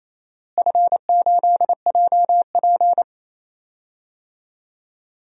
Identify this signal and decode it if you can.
Morse code, heard as a keyed tone, about 28 wpm, 700 Hz. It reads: F8JP
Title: DXCC-100 Call Signs - 4 Characters - Repeated 2x - 28wpm